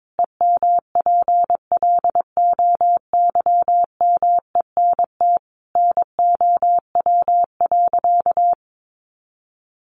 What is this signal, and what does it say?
Morse code, heard as a keyed tone, about 22 wpm, 700 Hz. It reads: EMPLOYMENT NOW.
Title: Sentences from Top 1500 Words 22wpm